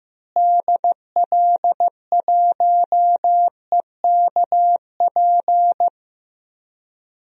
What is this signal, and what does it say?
Morse code, heard as a keyed tone, about 15 wpm, 700 Hz. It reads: DL1EKP
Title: DXCC-100 Call Signs - 6 Characters - Repeated 2x - 15wpm